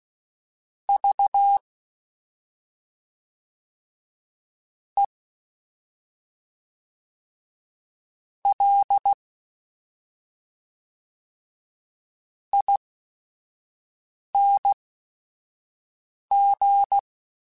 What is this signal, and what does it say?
Morse code, heard as a keyed tone, about 16 wpm, spaced as for 5 wpm, 800 Hz. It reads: V E L ING
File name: Special INGs / Special INGs-mp3s / SlowToFaster ing @5